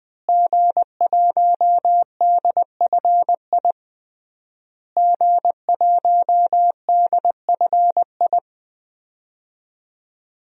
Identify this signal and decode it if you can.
Morse code, heard as a keyed tone, about 20 wpm, 700 Hz. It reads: G1DFI G1DFI
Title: DXCC-100 Call Signs - 5 Characters - Repeated 3x - 20wpm